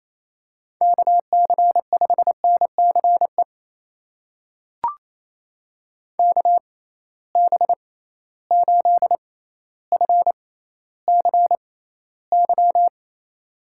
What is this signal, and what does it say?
Morse code, heard as a keyed tone, about 28 wpm, spaced as for 14 wpm, 700 Hz. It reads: KC5NCE KB8FCY
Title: USA Call Signs - 6 Characters - 6x Character Spacing - 28wpm